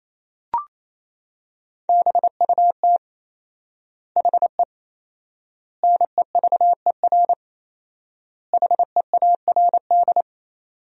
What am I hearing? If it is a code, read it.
Morse code, heard as a keyed tone, about 28 wpm, 700 Hz. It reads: BUT HE NEVER HEARD
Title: 4x Word-Spacing - Sentences from Top 400 Words - 28wpm